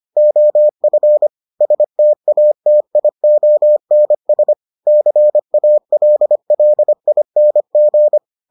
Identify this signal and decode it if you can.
Morse code, heard as a keyed tone, about 25 wpm, 600 Hz. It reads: OF STATIONS CALLING